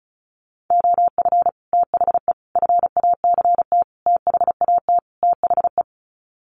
Morse code, heard as a keyed tone, about 34 wpm, 700 Hz. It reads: OF THE FACT THAT THE
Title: Sets of 5 Words Encore2 35wpm